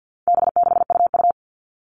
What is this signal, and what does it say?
Morse code, heard as a keyed tone, about 50 wpm, 700 Hz. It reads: B6UU
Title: All Cut Numbers - 4-Digit 50wpm